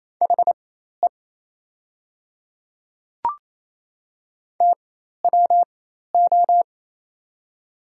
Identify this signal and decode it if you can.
Morse code, heard as a keyed tone, about 28 wpm, spaced as for 13 wpm, 700 Hz. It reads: HE TWO
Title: 4x Character Spacing - Top 300 Words - 28wpm